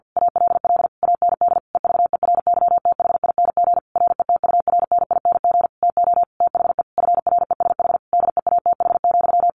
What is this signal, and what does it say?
Morse code, heard as a keyed tone, about 50 wpm, 700 Hz. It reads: ALL AND EVERYTHING RETURNING TO THE FRESH DEATH,
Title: A Sound Of Thunder 50wpm